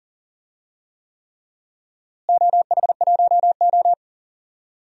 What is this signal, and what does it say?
Morse code, heard as a keyed tone, about 40 wpm, 700 Hz. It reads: OH1O
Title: DXCC-100 Call Signs - 4 Characters - Repeated 3x - 40wpm